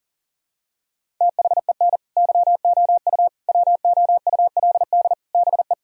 Morse code, heard as a keyed tone, about 37 wpm, 700 Hz. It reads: THEN YOU WOULD BE
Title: Sentences from Top 1000 Words - Review Part 1 - 40wpm